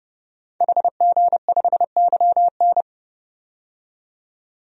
Morse code, heard as a keyed tone, about 30 wpm, 700 Hz. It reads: HG5YN
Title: DXCC-100 Call Signs - 5 Characters - 30wpm